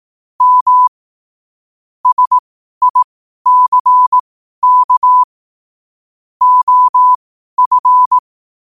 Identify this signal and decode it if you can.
Morse code, heard as a keyed tone, about 18 wpm, spaced as for 12 wpm, 1000 Hz. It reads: M SICK OF
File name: qotdfiles / 12wpm